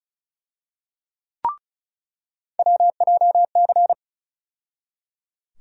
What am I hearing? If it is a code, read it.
Morse code, heard as a keyed tone, about 34 wpm, 700 Hz. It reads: WJC
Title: USA Call Sign Postfixes - 3 Letters 35wpm